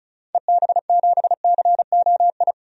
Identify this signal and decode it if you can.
Morse code, heard as a keyed tone, about 36 wpm, 700 Hz. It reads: EB7COI
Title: DXCC-100 Call Signs - 6 Characters - 35wpm